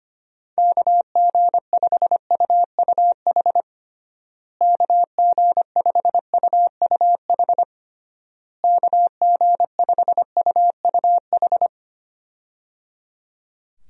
Morse code, heard as a keyed tone, about 25 wpm, 700 Hz. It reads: KG5UUH KG5UUH KG5UUH
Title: USA Call Signs - 6 Characters - Repeated 3x - 25wpm